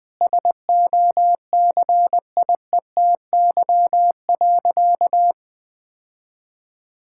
Morse code, heard as a keyed tone, about 20 wpm, 700 Hz. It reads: SOCIETY.
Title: Sentences from Top 1000 Words - Review Part 6 - 20wpm